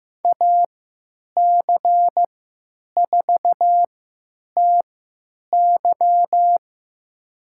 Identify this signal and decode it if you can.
Morse code, heard as a keyed tone, about 15 wpm, spaced as for 9 wpm, 700 Hz. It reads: AC4TY
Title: USA Call Signs - 5 Characters - 3x Character Spacing - 15wpm